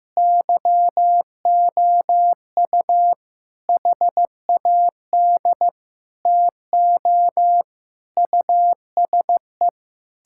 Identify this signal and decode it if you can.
Morse code, heard as a keyed tone, about 16 wpm, 700 Hz. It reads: YOU HAD TO USE
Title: Sentences from Top 1000 Words - Review Part 6 - 15wpm